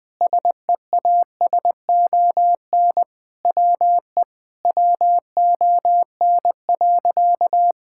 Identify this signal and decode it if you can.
Morse code, heard as a keyed tone, about 21 wpm, 700 Hz. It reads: SEASON WE WON.
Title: Sentences with 7-Word Phrases - Spoken Beginning - 20wpm